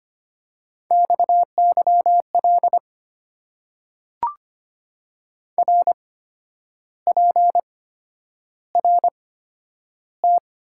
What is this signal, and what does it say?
Morse code, heard as a keyed tone, about 25 wpm, spaced as for 10 wpm, 700 Hz. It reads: XYL RPRT
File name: Top 90 QSO Elements - 8x Character Spacing - 25wpm